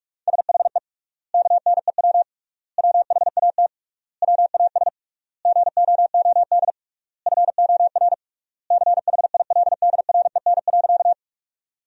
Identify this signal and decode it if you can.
Morse code, heard as a keyed tone, about 46 wpm, 700 Hz. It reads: SHE KNEW WHAT WAS GOOD FOR CHILDREN.
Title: 3x Word-Spacing - Sentences from Top 400 Words - 45wpm